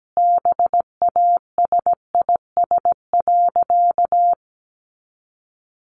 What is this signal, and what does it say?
Morse code, heard as a keyed tone, about 17 wpm, 700 Hz. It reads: BASIS.
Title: Sentences from Top 1500 Words 17wpm